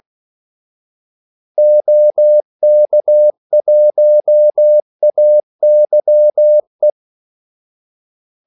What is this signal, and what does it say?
Morse code, heard as a keyed tone, about 16 wpm, 600 Hz. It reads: OK1AYE
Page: Kaldesignaler 16 wpm